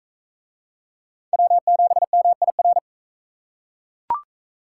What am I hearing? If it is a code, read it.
Morse code, heard as a keyed tone, about 36 wpm, 700 Hz. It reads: W7MIR